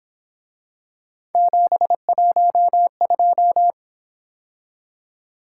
Morse code, heard as a keyed tone, about 26 wpm, 700 Hz. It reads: 712
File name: Speed-Racing - Three Letter-Number Combinations - 22wpm